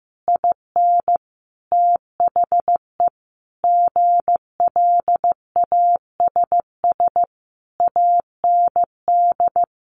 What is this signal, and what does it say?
Morse code, heard as a keyed tone, about 15 wpm, 700 Hz. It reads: IN THE GLASS AND
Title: Sentences from Top 1300 Words 15wpm